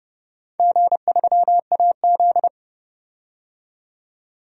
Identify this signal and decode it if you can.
Morse code, heard as a keyed tone, about 30 wpm, 700 Hz. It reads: G3AZ
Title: DXCC-100 Call Signs - 4 Characters - 30wpm